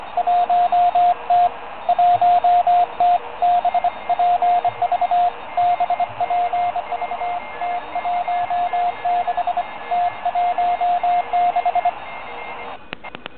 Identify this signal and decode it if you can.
Morse code, heard as a keyed tone, about 24 wpm, 710 Hz, with radio noise. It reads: 1T 1T BPV BPV T16 T16
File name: OK QRP 2017 / OKQRP2017 Zvuk / REC61